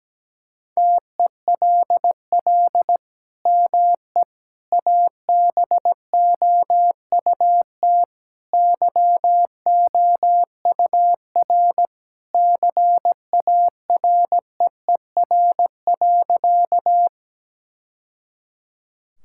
Morse code, heard as a keyed tone, about 17 wpm, 700 Hz. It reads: TELL ME ABOUT YOUR CAREER.
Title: Sentences from Top 1000 Words - Review Part 6 - 17wpm